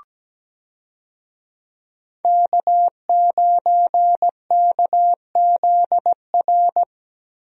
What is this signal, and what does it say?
Morse code, heard as a keyed tone, about 17 wpm, 700 Hz. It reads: K9KZR
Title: USA Call Signs - 5 Characters - Repeated 2x - 17wpm